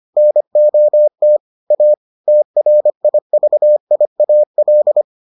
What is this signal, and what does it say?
Morse code, heard as a keyed tone, about 25 wpm, 600 Hz. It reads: NOT A TRIVIAL